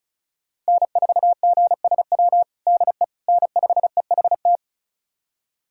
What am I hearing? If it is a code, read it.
Morse code, heard as a keyed tone, about 36 wpm, 700 Hz. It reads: N4GSW DE N5EHT
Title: USA Callsign DE Callsign - 5 Characters - 35wpm